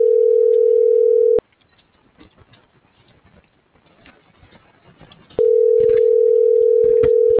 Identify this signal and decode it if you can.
Ringback tone (North America), 440 + 480 Hz.